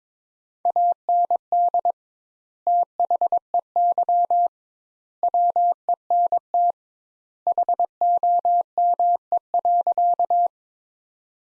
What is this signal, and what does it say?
Morse code, heard as a keyed tone, about 22 wpm, 700 Hz. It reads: AND THEY WENT HOME.
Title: Sentences Top 200 Words - 22wpm